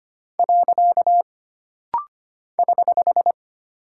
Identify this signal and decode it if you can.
Morse code, heard as a keyed tone, about 25 wpm, 700 Hz. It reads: . <HH>
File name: Single Punctuation-Prosign - Rapid-Fire - 25wpm